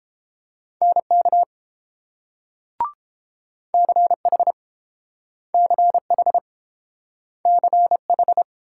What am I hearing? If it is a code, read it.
Morse code, heard as a keyed tone, about 32 wpm, 700 Hz. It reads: NK CH CH CH